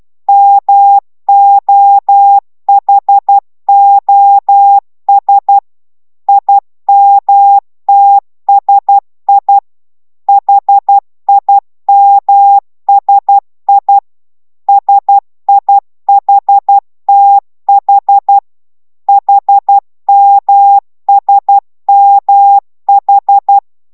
Morse code, heard as a keyed tone, about 12 wpm, 800 Hz. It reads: MOHOS IMTSI HIMSI SIHTH HMSMH